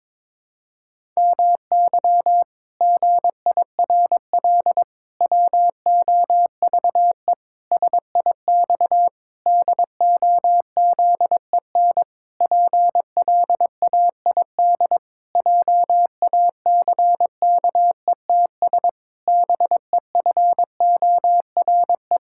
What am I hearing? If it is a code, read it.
Morse code, heard as a keyed tone, about 22 wpm, 700 Hz. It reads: MY GIRL WOVE SIX DOZEN PLAID JACKETS BEFORE
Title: Pangram Challenge v2 - 22wpm